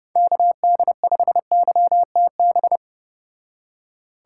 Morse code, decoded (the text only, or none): KD5YTB